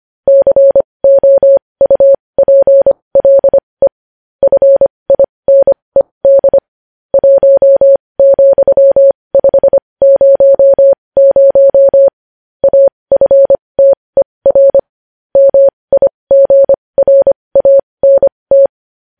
Morse code, heard as a keyed tone, about 25 wpm, spaced as for 20 wpm, 550 Hz. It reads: COUPLE FINED 1,500 AFTER MIGRANT
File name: podcast / News headlines-1x-25x20